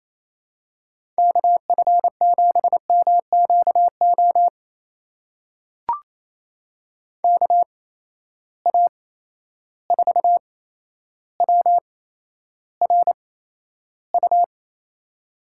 Morse code, heard as a keyed tone, about 28 wpm, spaced as for 12 wpm, 700 Hz. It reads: KF7MQO KA4WRU
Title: USA Call Signs - 6 Characters - 8x Character Spacing - 28wpm